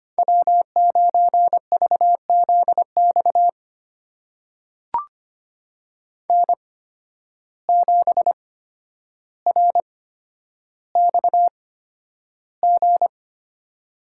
Morse code, heard as a keyed tone, about 25 wpm, spaced as for 11 wpm, 700 Hz. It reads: W9VZX N7RXG
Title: USA Call Signs - 5 Characters - 8x Character Spacing - 25wpm